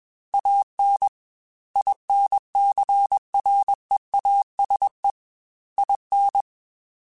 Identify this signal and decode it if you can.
Morse code, heard as a keyed tone, about 21 wpm, 800 Hz. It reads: AN INCREASE IN
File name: Technical Subjects / Technical Subjects-mp3s / XC @20